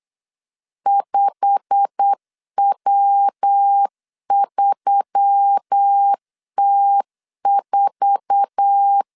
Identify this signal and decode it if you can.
Morse code, heard as a keyed tone, about 8 wpm, 800 Hz. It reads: 5W3T4
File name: CW / DET / 1 9 1500C S B 8wpm 5624